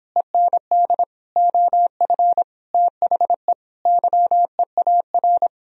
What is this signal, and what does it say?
Morse code, heard as a keyed tone, about 26 wpm, 700 Hz. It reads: END OF THE YEAR